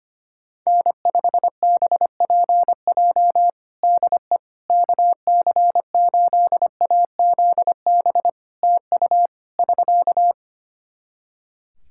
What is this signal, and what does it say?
Morse code, heard as a keyed tone, about 25 wpm, 700 Hz. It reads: N5BPJ DE KC8AZB TU <SK>
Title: K1USN SST QSO Practice 25wpm